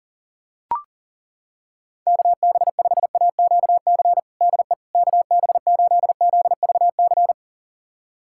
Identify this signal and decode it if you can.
Morse code, heard as a keyed tone, about 40 wpm, 700 Hz. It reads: KB5AQC DE KB8ZVC
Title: USA Callsign DE Callsign - 6 Characters - 40wpm